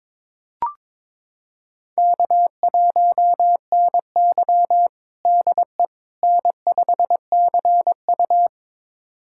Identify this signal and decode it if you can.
Morse code, heard as a keyed tone, about 22 wpm, 700 Hz. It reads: K1NY DE N5CU
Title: USA Callsign DE Callsign - 4 Characters - 22wpm